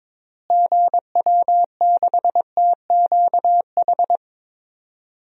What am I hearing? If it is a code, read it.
Morse code, heard as a keyed tone, about 22 wpm, 700 Hz. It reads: GW6TQH
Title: DXCC-100 Call Signs - 6 Characters - Repeated 2x - 22wpm